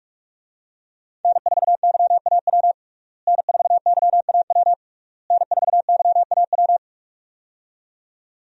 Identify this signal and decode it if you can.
Morse code, heard as a keyed tone, about 41 wpm, 700 Hz. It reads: N4YAW N4YAW N4YAW
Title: USA Call Signs - 5 Characters - Repeated 3x - 45wpm